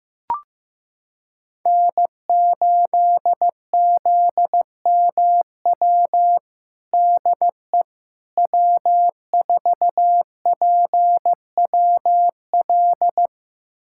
Morse code, heard as a keyed tone, about 15 wpm, 700 Hz. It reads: N8ZMW DE W4PWL
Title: USA Callsign DE Callsign - 5 Characters - 15wpm